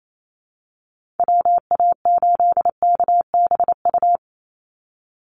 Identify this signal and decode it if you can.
Morse code, heard as a keyed tone, about 28 wpm, 700 Hz. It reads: WA8KBU
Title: Call Signs Repeated 2x - 28wpm